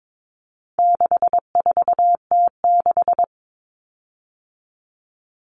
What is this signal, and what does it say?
Morse code, heard as a keyed tone, about 22 wpm, 700 Hz. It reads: 64T6
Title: Common Cut Numbers - 4-Digit 22wpm